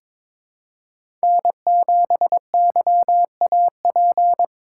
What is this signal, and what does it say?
Morse code, heard as a keyed tone, about 22 wpm, 700 Hz. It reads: N7YAP
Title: USA Call Signs - 5 Characters - 2x Character Spacing - 22wpm